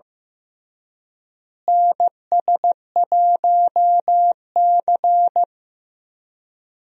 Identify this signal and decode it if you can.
Morse code, heard as a keyed tone, about 15 wpm, 700 Hz. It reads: NS1C
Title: USA Call Signs - 4 Characters - Repeated 3x - 15wpm